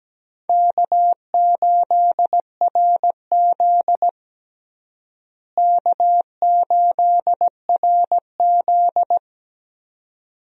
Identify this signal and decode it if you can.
Morse code, heard as a keyed tone, about 17 wpm, 700 Hz. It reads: K8RZ K8RZ
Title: USA Call Signs - 4 Characters - Repeated 2x - 17wpm